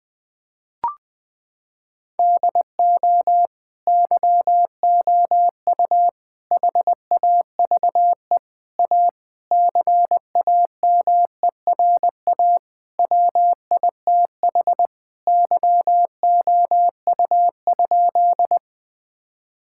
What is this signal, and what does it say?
Morse code, heard as a keyed tone, about 20 wpm, 700 Hz. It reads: DO YOU HAVE A CAMERA WITH YOU?
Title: Sentences from Top 1000 Words - Review Part 5 - 20wpm